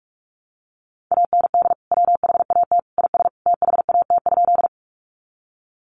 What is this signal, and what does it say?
Morse code, heard as a keyed tone, about 44 wpm, 700 Hz. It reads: AND WHAT IS THAT?
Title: Sentences from Top 100 Words 45wpm